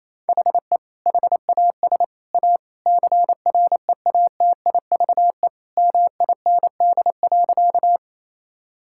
Morse code, heard as a keyed tone, about 28 wpm, 700 Hz. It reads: HE HAS A CREATIVE MIND.
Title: Sentences from Top 1000 Words - Review Part 5 - 28wpm